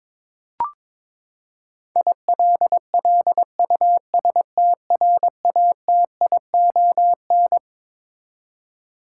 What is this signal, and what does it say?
Morse code, heard as a keyed tone, about 22 wpm, 700 Hz. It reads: ILLUSTRATION
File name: Twelve Letter Words - 22wpm